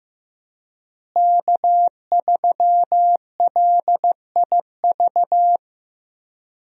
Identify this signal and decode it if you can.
Morse code, heard as a keyed tone, about 15 wpm, 700 Hz. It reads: K3LIV
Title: USA Call Signs - 5 Characters - Repeated 3x - 15wpm